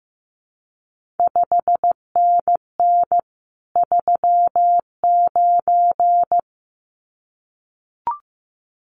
Morse code, heard as a keyed tone, about 15 wpm, 700 Hz. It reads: 5NN 39
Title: CQ WW DX Contest Exchange - 15wpm